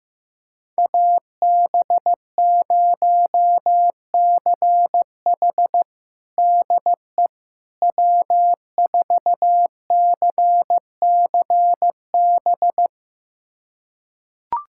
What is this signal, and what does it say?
Morse code, heard as a keyed tone, about 15 wpm, 700 Hz. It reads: AB0CH DE W4CCB
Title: USA Callsign DE Callsign - 5 Characters - 15wpm